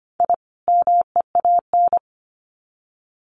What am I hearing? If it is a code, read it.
Morse code, heard as a keyed tone, about 25 wpm, 700 Hz. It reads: I MEAN
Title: Sets of 2 Words 25wpm